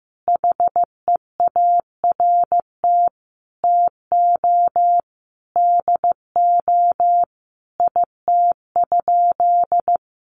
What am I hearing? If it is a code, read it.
Morse code, heard as a keyed tone, about 14 wpm, 700 Hz. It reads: HEART TO DO IT?
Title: Sentences from Top 1200 Words 15wpm